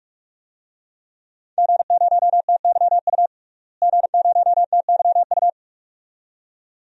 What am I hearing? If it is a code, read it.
Morse code, heard as a keyed tone, about 46 wpm, 700 Hz. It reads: G0TYU G0TYU
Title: DXCC-100 Call Signs - 5 Characters - Repeated 2x - 45wpm